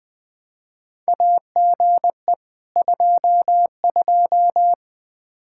Morse code, heard as a keyed tone, about 20 wpm, 700 Hz. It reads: AGE 22
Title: Practice QSO - Exchange 3 - More Info - 20wpm